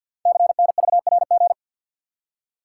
Morse code, heard as a keyed tone, about 52 wpm, 700 Hz. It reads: CNVRG
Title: Code Groups - 5 Letters - 50wpm